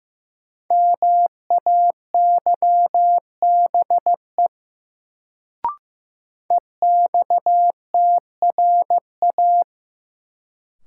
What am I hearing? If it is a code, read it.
Morse code, heard as a keyed tone, about 15 wpm, 700 Hz. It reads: MAYBE EXTRA
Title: Sentences from Top 1000 Words - New Word Drill - 15wpm